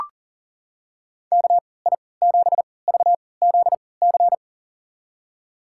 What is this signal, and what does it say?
Morse code, heard as a keyed tone, about 40 wpm, spaced as for 23 wpm, 700 Hz. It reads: KI7VZC
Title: USA Call Signs - 6 Characters - 3x Character Spacing - 40wpm